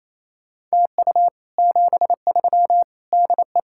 Morse code, heard as a keyed tone, about 36 wpm, 700 Hz. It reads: TU 73 DE